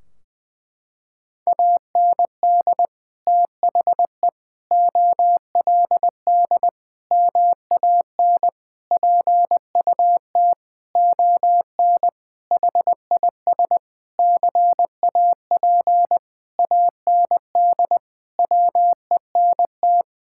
Morse code, heard as a keyed tone, about 20 wpm, 700 Hz. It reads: AND THE OLD MAN PUT ON HIS CAP AND WENT
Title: Sentences from Top 1600 Words 20wpm